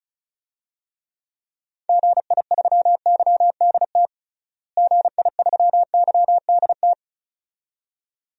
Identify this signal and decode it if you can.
Morse code, heard as a keyed tone, about 36 wpm, 700 Hz. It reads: GI3YDT GI3YDT
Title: DXCC-100 Call Signs - 6 Characters - Repeated 2x - 35wpm